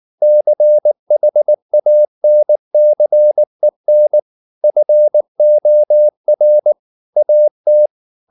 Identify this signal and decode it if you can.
Morse code, heard as a keyed tone, about 19 wpm, 600 Hz. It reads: CHANCEN FOR AT